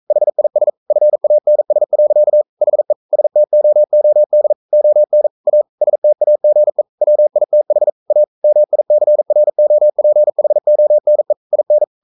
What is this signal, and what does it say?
Morse code, heard as a keyed tone, about 42 wpm, 600 Hz. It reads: HIS FANS. HE STOOD ON A STAGE WITH A MICROPHONE IN
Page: Romanen Never af Ken Follet på engelsk.